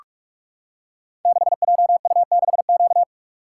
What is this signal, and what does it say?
Morse code, heard as a keyed tone, about 44 wpm, 700 Hz. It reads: 6JU6Q
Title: Code Groups - 5 Letter:Numbers - 45wpm